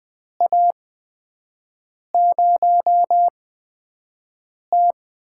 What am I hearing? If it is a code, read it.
Morse code, heard as a keyed tone, about 20 wpm, spaced as for 5 wpm, 700 Hz. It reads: A0T
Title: USA Call Signs - 4 Characters - 8x Character Spacing - 20wpm